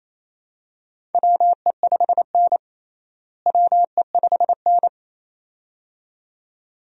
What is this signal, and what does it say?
Morse code, heard as a keyed tone, about 28 wpm, 700 Hz. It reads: WE5N WE5N
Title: USA Call Signs - 4 Characters - Repeated 2x - 28wpm